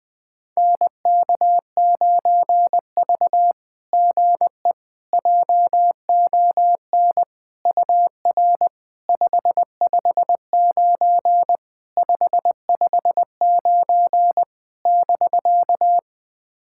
Morse code, heard as a keyed tone, about 20 wpm, 700 Hz. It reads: NK9V GE JON UR 559 559 <BK>
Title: SOTA QSO Practice 20wpm